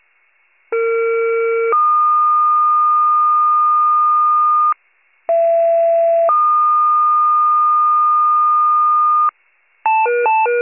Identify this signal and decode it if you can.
Two-tone page, heard as a one-second tone, then a three-second tone, about 460 then 1220 Hz, about 670 then 1220 Hz. It is heard from 0.7 to 9.3 s.